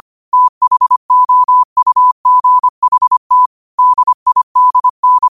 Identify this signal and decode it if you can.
Morse code, heard as a keyed tone, about 25 wpm, 1000 Hz. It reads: THOUGHT DIDN
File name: qotdfiles / 25wpm